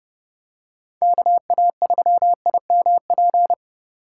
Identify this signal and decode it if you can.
Morse code, heard as a keyed tone, about 31 wpm, 700 Hz. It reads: KA3IMP
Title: USA Call Signs - 6 Characters - 3x Character Spacing - 30wpm